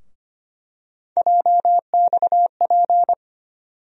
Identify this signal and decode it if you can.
Morse code, heard as a keyed tone, about 25 wpm, 700 Hz. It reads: JXP
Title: USA Call Sign Postfixes - 3 Letters 25wpm